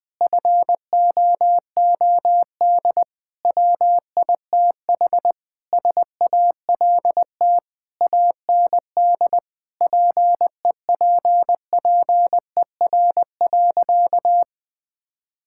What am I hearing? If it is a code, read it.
Morse code, heard as a keyed tone, about 21 wpm, 700 Hz. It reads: FOOD WITH SALT AND PEPPER.
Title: Sentences with 5-Word Phrases - 20wpm